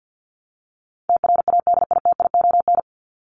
Morse code, heard as a keyed tone, about 54 wpm, 700 Hz. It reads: TRADITION
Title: Top 1251-1500 Words 50wpm